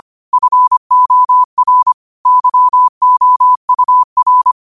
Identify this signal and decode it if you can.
Morse code, heard as a keyed tone, about 25 wpm, 1000 Hz. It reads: FOR YOUR
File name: qotdfiles / 25wpm